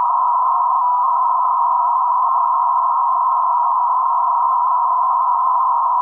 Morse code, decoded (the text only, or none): none